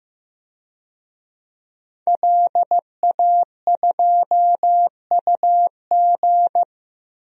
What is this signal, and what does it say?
Morse code, heard as a keyed tone, about 15 wpm, 700 Hz. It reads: LA2UG
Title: DXCC-100 Call Signs - 5 Characters - Repeated 2x - 15wpm